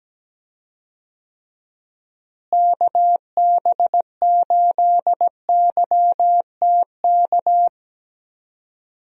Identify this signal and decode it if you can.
Morse code, heard as a keyed tone, about 17 wpm, 700 Hz. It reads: KB8YTK
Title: USA Call Signs - 6 Characters - Repeated 2x - 17wpm